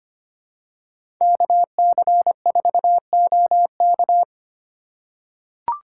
Morse code, heard as a keyed tone, about 25 wpm, 700 Hz. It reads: KC4OK
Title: USA Call Signs - 5 Characters - 2x Character Spacing - 25wpm